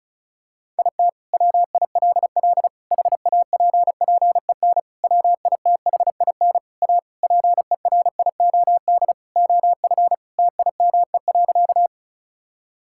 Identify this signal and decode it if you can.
Morse code, heard as a keyed tone, about 36 wpm, 700 Hz. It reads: IT WILL HAPPEN WITHIN A PERIOD OF TIME.
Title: Sentences from Top 1000 Words - Review Part 5 - 35wpm